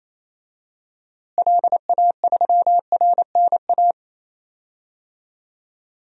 Morse code, heard as a keyed tone, about 28 wpm, 700 Hz. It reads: LA3RNA
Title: DXCC-100 Call Signs - 6 Characters - Repeated 2x - 28wpm